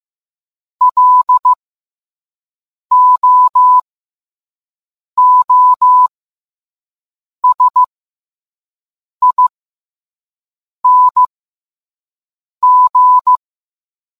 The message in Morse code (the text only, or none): LOOSING